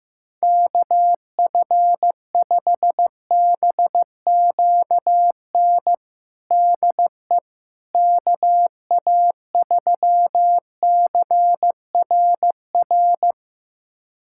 Morse code, heard as a keyed tone, about 15 wpm, 700 Hz. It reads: KF5BQN DE KA3CRR
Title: USA Callsign DE Callsign - 6 Characters - 15wpm